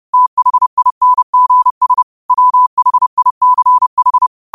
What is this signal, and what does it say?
Morse code, heard as a keyed tone, about 30 wpm, 1000 Hz. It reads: THINGS WHICH
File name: qotdfiles / 30wpm